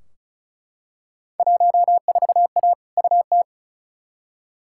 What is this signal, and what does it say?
Morse code, heard as a keyed tone, about 36 wpm, 700 Hz. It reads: 14A UT
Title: Field Day Exchanges Encore - 35wpm